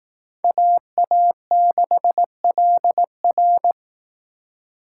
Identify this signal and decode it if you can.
Morse code, heard as a keyed tone, about 18 wpm, 700 Hz. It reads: AA6LR
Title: Speed Racing - USA 2x2 Call Signs 15wpm